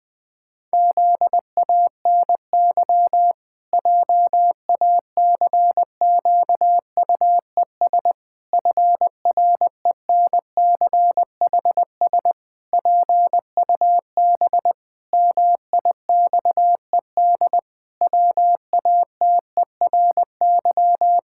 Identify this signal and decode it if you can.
Morse code, heard as a keyed tone, about 21 wpm, 700 Hz. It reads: ZANY JACQUES FRENCHS PUB MIXED WATERY
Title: Pangram Challenge v2 - 20wpm